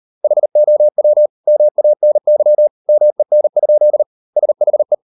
Morse code, heard as a keyed tone, about 38 wpm, 600 Hz. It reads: HOW MANY MEN? SHE